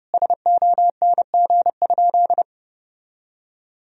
Morse code, heard as a keyed tone, about 31 wpm, 700 Hz. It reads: SONG?
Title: Sentences from Top 1000 Words - Review Part 2 - 30wpm